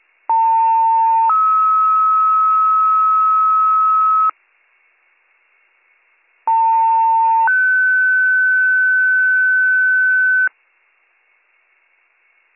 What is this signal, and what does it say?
Two-tone page, heard as a one-second tone, then a three-second tone, about 900 then 1290 Hz, about 900 then 1500 Hz. It is heard from 0.3 to 10.5 s.